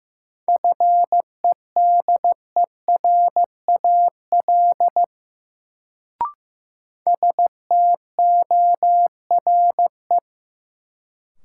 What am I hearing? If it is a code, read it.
Morse code, heard as a keyed tone, about 15 wpm, 700 Hz. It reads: FEDERAL STORE
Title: Sentences from Top 1000 Words - Review Part 5 - New Word Drill - 15wpm